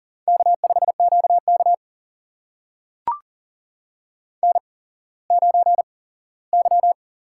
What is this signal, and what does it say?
Morse code, heard as a keyed tone, about 39 wpm, spaced as for 19 wpm, 700 Hz. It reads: K5QK N9Y
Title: USA Call Signs - 8x Character Spacing - 40wpm